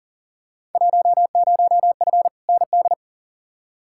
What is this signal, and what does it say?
Morse code, heard as a keyed tone, about 38 wpm, 700 Hz. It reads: 10F ND
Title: Field Day Exchanges Encore - 40wpm